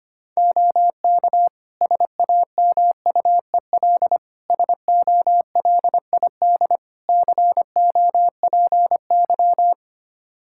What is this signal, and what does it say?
Morse code, heard as a keyed tone, about 25 wpm, 700 Hz. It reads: OK SAMUEL SOLID COPY
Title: K1USN SST QSO Practice 25wpm